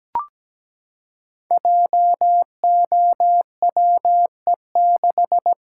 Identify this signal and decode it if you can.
Morse code, heard as a keyed tone, about 17 wpm, 700 Hz. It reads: JOWE6
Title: Code Groups - 5 Letter:Numbers - 17wpm